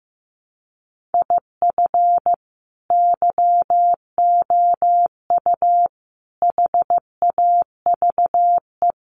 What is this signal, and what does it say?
Morse code, heard as a keyed tone, about 15 wpm, 700 Hz. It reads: IF YOU HAVE
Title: Sets of 4 Words 15wpm